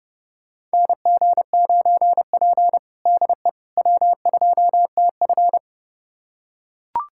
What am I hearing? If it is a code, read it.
Morse code, heard as a keyed tone, about 30 wpm, 700 Hz. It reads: NG9P DE W2TF
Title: USA Callsign DE Callsign - 4 Characters - 30wpm